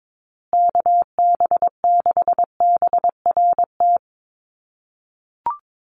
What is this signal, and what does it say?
Morse code, heard as a keyed tone, about 22 wpm, 700 Hz. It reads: KB6BRT
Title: USA Call Signs Repeated 3x 22wpm